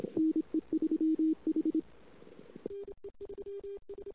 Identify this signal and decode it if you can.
Morse code, heard as a keyed tone, about 26 wpm, 320 Hz. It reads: NE3H
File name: CW / 2008-ss-cw-k5tr-op-n5rz -19:00